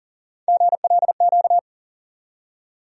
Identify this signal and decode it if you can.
Morse code, heard as a keyed tone, about 40 wpm, 700 Hz. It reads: GLQ
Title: USA Call Sign Postfixes - 3 Letters 40wpm